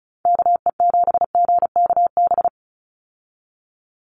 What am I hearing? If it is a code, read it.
Morse code, heard as a keyed tone, about 36 wpm, 700 Hz. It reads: KE7GKB
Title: USA Call Signs 35wpm